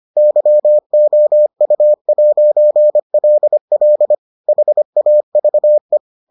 Morse code, heard as a keyed tone, about 25 wpm, 600 Hz. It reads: YOU'LL HAVE